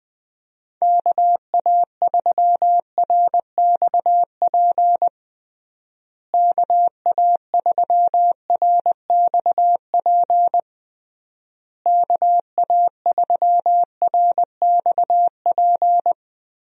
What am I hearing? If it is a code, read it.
Morse code, heard as a keyed tone, about 20 wpm, 700 Hz. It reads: KA3RXP KA3RXP KA3RXP
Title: USA Call Signs - 6 Characters - Repeated 3x - 20wpm